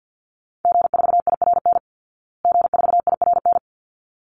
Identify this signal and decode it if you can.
Morse code, heard as a keyed tone, about 50 wpm, 700 Hz. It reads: G4IRN G4IRN
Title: Call Signs Repeated 3x - 50wpm